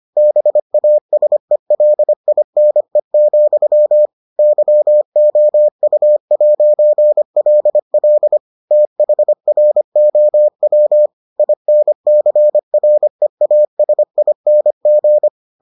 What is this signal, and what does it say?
Morse code, heard as a keyed tone, about 25 wpm, 600 Hz. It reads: BASELINE, YOU'LL THROW INCREASING